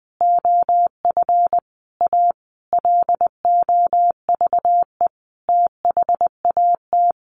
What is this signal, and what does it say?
Morse code, heard as a keyed tone, about 20 wpm, 700 Hz. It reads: OF A LOVE THAT
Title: Sentences from Top 1500 Words 20wpm